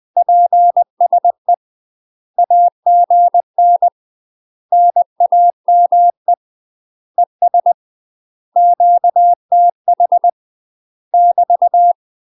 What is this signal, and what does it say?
Morse code, heard as a keyed tone, about 20 wpm, 700 Hz. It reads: PSE AGN NAME ES QTH <BT>
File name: UK MASTER QSOs TRAINING / 20WPM / UK MASTER QSOs 0006